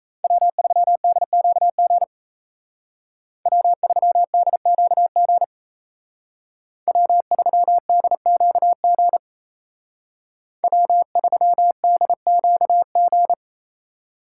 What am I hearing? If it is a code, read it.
Morse code, heard as a keyed tone, about 33 wpm, 700 Hz. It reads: W3DQG W3DQG W3DQG W3DQG
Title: Speed Racing - USA 1x3 Call Signs 28wpm